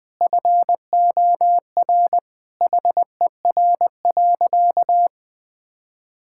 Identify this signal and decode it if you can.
Morse code, heard as a keyed tone, about 19 wpm, 700 Hz. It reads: FOR HER.
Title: Sentences from Top 1000 Words - Review Part 6 - 20wpm